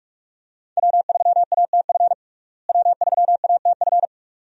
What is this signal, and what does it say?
Morse code, heard as a keyed tone, about 45 wpm, 700 Hz. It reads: W3ATF W3ATF
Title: USA Call Signs - 5 Characters - Repeated 2x - 45wpm